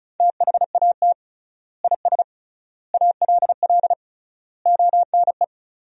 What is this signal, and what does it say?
Morse code, heard as a keyed tone, about 34 wpm, 700 Hz. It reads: THAT IS ALL ONE
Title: 3x Word-Spacing - Sentences from Top 200 Words - 35wpm